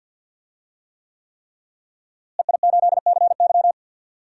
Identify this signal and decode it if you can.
Morse code, heard as a keyed tone, about 50 wpm, 700 Hz. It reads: EI8CY
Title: DXCC-100 Call Signs - 5 Characters - Repeated 3x - 50wpm